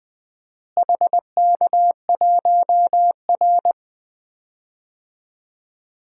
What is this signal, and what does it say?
Morse code, heard as a keyed tone, about 20 wpm, 700 Hz. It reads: HK1R
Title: DXCC-100 Call Signs - 4 Characters - Repeated 3x - 20wpm